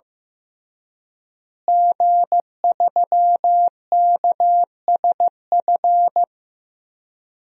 Morse code, heard as a keyed tone, about 15 wpm, 700 Hz. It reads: G3KSF
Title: DXCC-100 Call Signs - 5 Characters - Repeated 3x - 15wpm